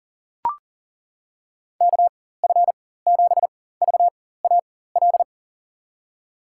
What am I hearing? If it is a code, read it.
Morse code, heard as a keyed tone, about 38 wpm, spaced as for 18 wpm, 700 Hz. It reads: KF7VAL
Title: USA Call Signs - 6 Characters - 4x Character Spacing - 40wpm